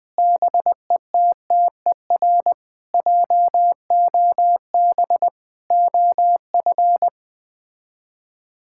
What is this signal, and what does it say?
Morse code, heard as a keyed tone, about 20 wpm, 700 Hz. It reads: BETTER JOB OF
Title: Sets of 6 Words - 20wpm